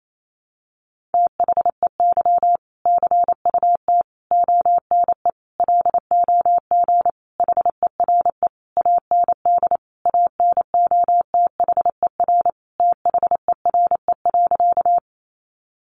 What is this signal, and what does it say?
Morse code, heard as a keyed tone, about 28 wpm, 700 Hz. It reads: THEY CUT ONE LOG HERE AND ANOTHER THERE.
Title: Sentences from Top 1400 Words 28wpm